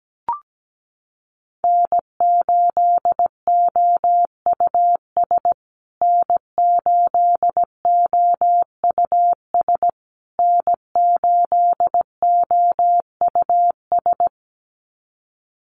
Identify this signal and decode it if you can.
Morse code, heard as a keyed tone, about 17 wpm, 700 Hz. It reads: N8OUS N8OUS N8OUS
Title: USA Call Signs Repeated 3x 17wpm